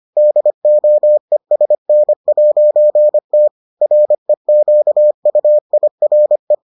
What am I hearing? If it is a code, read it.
Morse code, heard as a keyed tone, about 25 wpm, 600 Hz. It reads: DOESN'T REQUIRE